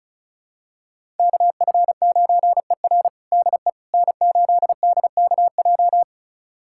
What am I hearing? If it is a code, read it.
Morse code, heard as a keyed tone, about 36 wpm, 700 Hz. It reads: KF9ER DE N8DKJ
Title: USA Callsign DE Callsign 35wpm